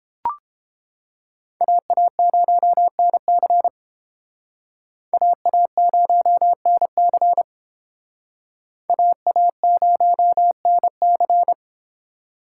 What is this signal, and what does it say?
Morse code, heard as a keyed tone, about 30 wpm, 700 Hz. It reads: AA0NC AA0NC AA0NC